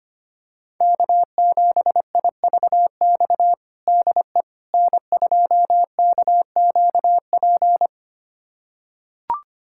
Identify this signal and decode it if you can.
Morse code, heard as a keyed tone, about 25 wpm, 700 Hz. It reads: K7IVX DE N2KQP
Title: USA Callsign DE Callsign - 5 Characters - 25wpm